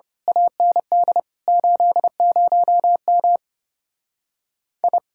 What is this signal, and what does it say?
Morse code, heard as a keyed tone, about 26 wpm, 700 Hz. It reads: AND 80M I
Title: Speed-Racing Common QSO Phrases - 25wpm